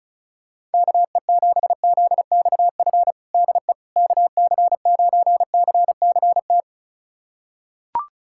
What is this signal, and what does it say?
Morse code, heard as a keyed tone, about 36 wpm, 700 Hz. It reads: KE7ZXF DE KC9CCT
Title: USA Callsign DE Callsign - 6 Characters - 35wpm